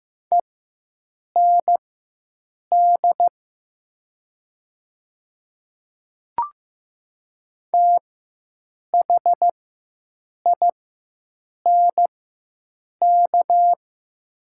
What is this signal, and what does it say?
Morse code, heard as a keyed tone, about 15 wpm, spaced as for 7 wpm, 700 Hz. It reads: END THINK
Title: 4x Character Spacing - Top 200 Words - 15wpm